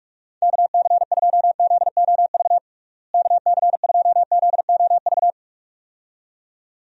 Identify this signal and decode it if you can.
Morse code, heard as a keyed tone, about 42 wpm, 700 Hz. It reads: KC2ZOV KC2ZOV
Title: USA Call Signs - 6 Characters - Repeated 2x - 45wpm